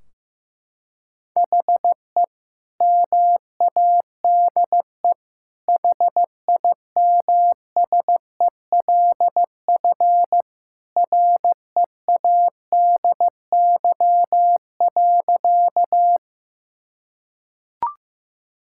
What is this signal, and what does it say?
Morse code, heard as a keyed tone, about 15 wpm, 700 Hz. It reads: HE MADE HIMSELF READY.
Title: Sentences from Top 1600 Words 15wpm